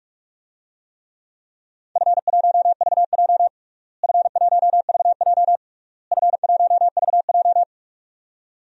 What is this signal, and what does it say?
Morse code, heard as a keyed tone, about 41 wpm, 700 Hz. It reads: F1VJ F1VJ F1VJ
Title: DXCC-100 Call Signs - 4 Characters - Repeated 3x - 45wpm